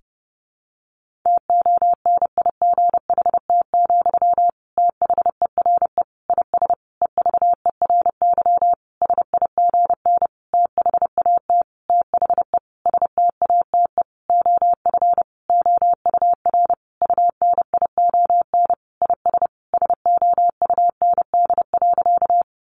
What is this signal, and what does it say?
Morse code, heard as a keyed tone, about 30 wpm, 700 Hz. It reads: TONIGHT, THERE IS EVERY SIGN THAT THE STATE OF OUR UNION IS SOUND.
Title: Sentences from Top 1400 Words 30wpm